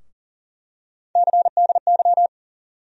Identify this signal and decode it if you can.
Morse code, heard as a keyed tone, about 40 wpm, 700 Hz. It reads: CDY
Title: USA Call Sign Postfixes - 3 Letters 40wpm